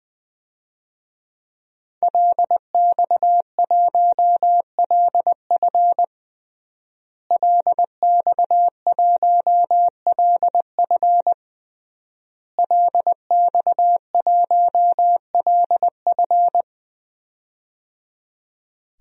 Morse code, heard as a keyed tone, about 20 wpm, 700 Hz. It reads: LX1LF LX1LF LX1LF
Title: DXCC-100 Call Signs - 5 Characters - Repeated 3x - 20wpm